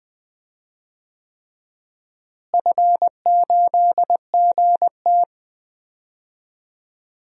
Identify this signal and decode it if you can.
Morse code, heard as a keyed tone, about 20 wpm, 700 Hz. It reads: F8GT
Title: DXCC-100 Call Signs - 4 Characters - Repeated 3x - 20wpm